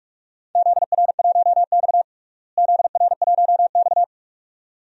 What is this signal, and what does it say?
Morse code, heard as a keyed tone, about 46 wpm, 700 Hz. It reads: ZR1X ZR1X
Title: DXCC-100 Call Signs - 4 Characters - Repeated 2x - 45wpm